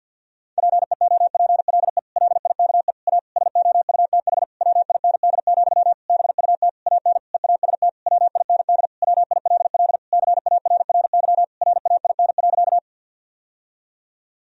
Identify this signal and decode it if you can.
Morse code, heard as a keyed tone, about 50 wpm, 700 Hz. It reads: PEOPLE LIKE A SOUTH WIND, BUT AN EAST WIND WILL CARRY RAIN.
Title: Sentences from Top 1000 Words - Review Part 3 - 50wpm